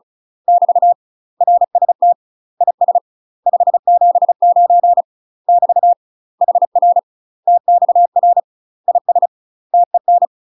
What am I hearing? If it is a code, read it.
Morse code, heard as a keyed tone, about 34 wpm, 700 Hz. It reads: <BT> RST IS 579 <BT> HR TXR IS TEN
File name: UK MASTER QSOs TRAINING / 35WPM / UK MASTER QSOs 0003